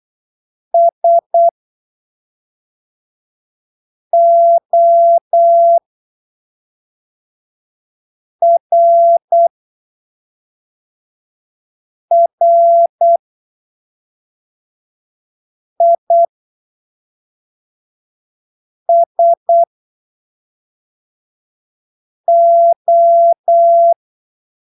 Morse code, heard as a keyed tone, about 8 wpm, spaced as for 3 wpm, 680 Hz. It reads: SORRISO
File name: arquivos / texto / texto 15 08ppm